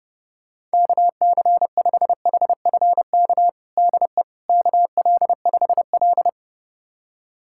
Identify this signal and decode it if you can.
Morse code, heard as a keyed tone, about 30 wpm, 700 Hz. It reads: KC5HFK DE KL5L
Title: USA Callsign DE Callsign 30wpm